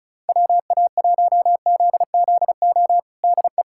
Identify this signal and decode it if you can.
Morse code, heard as a keyed tone, about 34 wpm, 700 Hz. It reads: WA1ZZO DE
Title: USA Callsign DE Callsign - 6 Characters - 35wpm